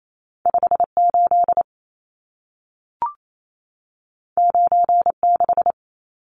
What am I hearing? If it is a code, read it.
Morse code, heard as a keyed tone, about 28 wpm, 700 Hz. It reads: 58 96
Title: Numbers - 2-Digit 28wpm